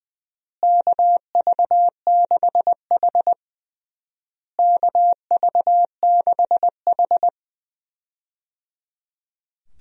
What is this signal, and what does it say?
Morse code, heard as a keyed tone, about 21 wpm, 700 Hz. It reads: KV6H KV6H
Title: USA Call Signs - 4 Characters - Repeated 3x - 20wpm